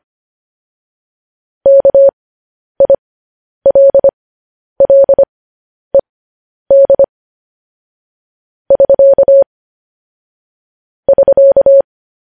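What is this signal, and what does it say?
Morse code, heard as a keyed tone, about 25 wpm, spaced as for 10 wpm, 550 Hz. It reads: KILLED <SK> <SK>
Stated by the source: Morse code news headlines sent at 10 words per minute.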